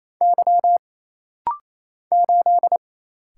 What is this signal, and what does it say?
Morse code, heard as a keyed tone, about 27 wpm, 700 Hz. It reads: Y 8
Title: Single Letter-Number Alt - Rapid-Fire 28wpm